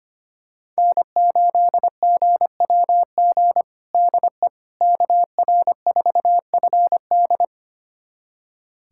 Morse code, heard as a keyed tone, about 25 wpm, 700 Hz. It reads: N8GWG DE KR4FD
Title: USA Callsign DE Callsign - 5 Characters - 25wpm